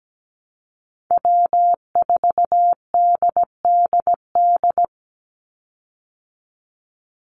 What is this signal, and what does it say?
Morse code, heard as a keyed tone, about 17 wpm, 700 Hz. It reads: W4DDD
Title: Call Signs Repeated 2x - 17wpm